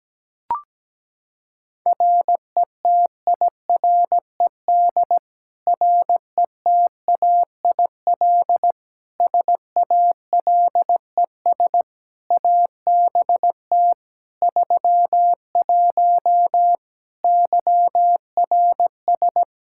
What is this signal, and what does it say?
Morse code, heard as a keyed tone, about 17 wpm, 700 Hz. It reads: RETIRED RETAIL SALES ABT 31 YRS
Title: Practice QSO - Exchange 3 - More Info - 17wpm